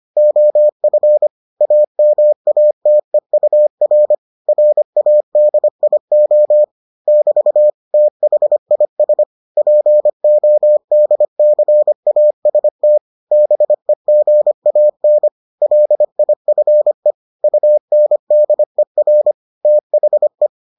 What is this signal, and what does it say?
Morse code, heard as a keyed tone, about 25 wpm, 600 Hz. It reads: OF AMATEUR RADIO <BT> THIS PODCAST BEGAN LIFE UNDER THE